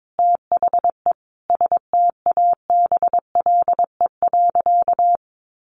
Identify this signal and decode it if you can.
Morse code, heard as a keyed tone, about 22 wpm, 700 Hz. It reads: THE STABLE.
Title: Sentences from Top 1500 Words 22wpm